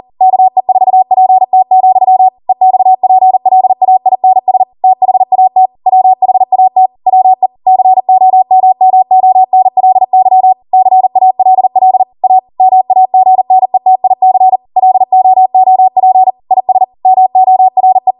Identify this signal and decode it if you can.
Morse code, heard as a keyed tone, about 40 wpm, 750 Hz. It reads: KE4PT, EXPLAINS THAT WHAT WE COMMONLY CALL A MAGNETIC LOOP IS MORE
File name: Morse / 40WPM